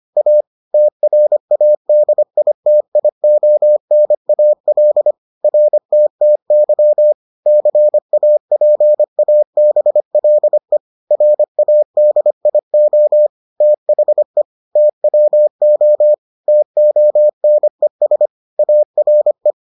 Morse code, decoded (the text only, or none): A TRADITIONAL RTTY CAPABLE RADIO THE TWO TONES ARE